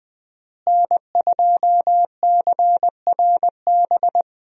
Morse code, heard as a keyed tone, about 20 wpm, 700 Hz. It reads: N2CRB
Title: USA Call Signs - 5 Characters - Repeated 2x - 20wpm